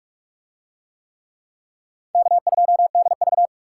Morse code, heard as a keyed tone, about 44 wpm, 700 Hz. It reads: K2DV